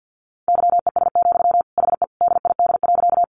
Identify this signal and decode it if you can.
Morse code, heard as a keyed tone, about 50 wpm, 700 Hz. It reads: YES, HE DID.
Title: Sentences from Top 500 Words 50wpm